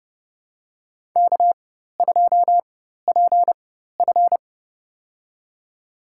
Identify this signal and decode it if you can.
Morse code, heard as a keyed tone, about 32 wpm, spaced as for 14 wpm, 700 Hz. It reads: K2PF
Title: USA Call Signs - 4 Characters - 4x Character Spacing - 30wpm